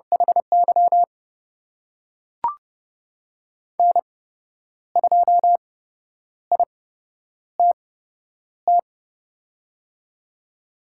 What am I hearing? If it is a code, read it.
Morse code, heard as a keyed tone, about 32 wpm, spaced as for 10 wpm, 700 Hz. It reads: HY N2ITT
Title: USA Call Signs - 8x Character Spacing - 30wpm